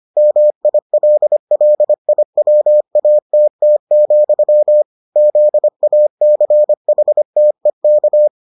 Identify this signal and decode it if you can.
Morse code, heard as a keyed tone, about 25 wpm, 600 Hz. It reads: MILLIWATT, ZACHTEK